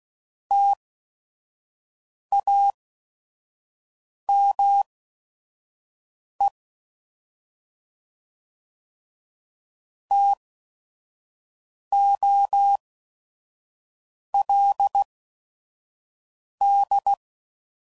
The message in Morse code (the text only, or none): TAME TOLD